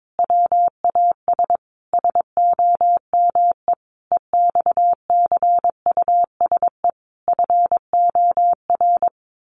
Morse code, heard as a keyed tone, about 22 wpm, 700 Hz. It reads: WAS SOME EXCUSE FOR
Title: Sentences from Top 700 Words 22wpm